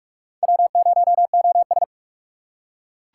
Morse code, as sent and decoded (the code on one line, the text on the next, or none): .-- ----- --- ...
W0OS